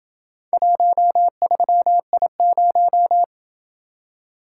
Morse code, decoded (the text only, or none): 13I0